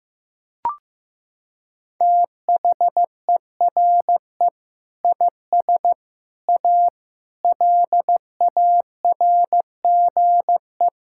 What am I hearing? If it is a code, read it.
Morse code, heard as a keyed tone, about 16 wpm, 700 Hz. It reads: THERE IS A LARGE
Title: Sentences from Top 1000 Words - Review Part 5 - 15wpm